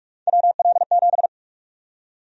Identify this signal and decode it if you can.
Morse code, heard as a keyed tone, about 41 wpm, 690 Hz. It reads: WL7
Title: USA Call Sign Prefixes - Repeated 2x - 45wpm